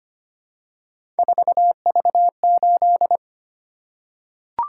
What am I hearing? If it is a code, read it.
Morse code, heard as a keyed tone, about 25 wpm, 700 Hz. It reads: 4V8
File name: Three Letter-Number Combinations - 25wpm